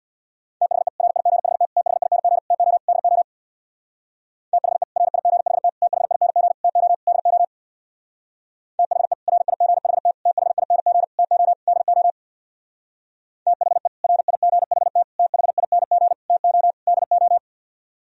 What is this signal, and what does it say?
Morse code, heard as a keyed tone, about 64 wpm, 700 Hz. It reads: THE RIGHT THING TO DO THE RIGHT THING TO DO THE RIGHT THING TO DO THE RIGHT THING TO DO